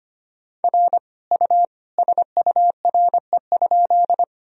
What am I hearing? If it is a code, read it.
Morse code, heard as a keyed tone, about 25 wpm, 700 Hz. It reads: R U SURE?
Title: Sentences from Top 300 Words and Abbreviations 25wpm